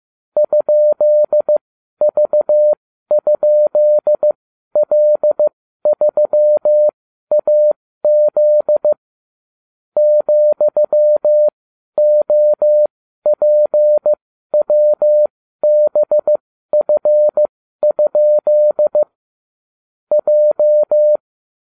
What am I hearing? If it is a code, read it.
Morse code, heard as a keyed tone, about 15 wpm, spaced as for 12 wpm, 600 Hz. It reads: ?V?L3AZ ,OPWBF? J